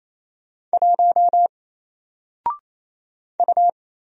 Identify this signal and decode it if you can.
Morse code, heard as a keyed tone, about 27 wpm, 700 Hz. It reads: 1 U
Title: Single Letter-Number 28wpm